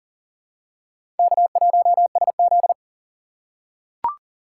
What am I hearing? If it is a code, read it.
Morse code, heard as a keyed tone, about 40 wpm, 690 Hz. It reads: K1SZ
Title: USA Call Signs - 2x Character Spacing - 40wpm